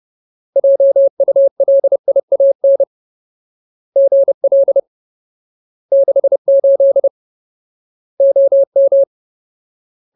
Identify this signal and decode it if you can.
Morse code, heard as a keyed tone, about 30 wpm, 550 Hz. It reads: JULIAN GL 68 OM
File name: Ultimate Word List / 30WPM Ultimate